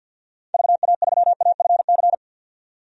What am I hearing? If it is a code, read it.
Morse code, heard as a keyed tone, about 48 wpm, 700 Hz. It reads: VA3AFC
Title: DXCC-100 Call Signs - 6 Characters - 50wpm